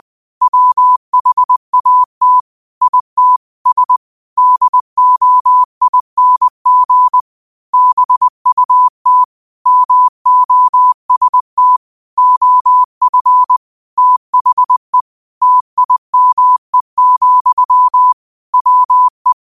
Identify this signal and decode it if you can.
Morse code, heard as a keyed tone, about 20 wpm, 1000 Hz. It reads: WHAT IT S DOING BUT MOST OF THE TIME, WE